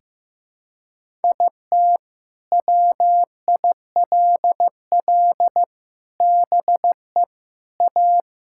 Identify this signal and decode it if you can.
Morse code, heard as a keyed tone, about 16 wpm, 700 Hz. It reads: IT WILL BE A
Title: Sentences from Top 1000 Words - Review Part 6 - 15wpm